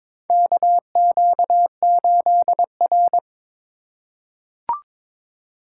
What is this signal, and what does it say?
Morse code, heard as a keyed tone, about 22 wpm, 700 Hz. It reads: KQ8R
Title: USA Call Signs - 4 Characters - 4x Character Spacing - 22wpm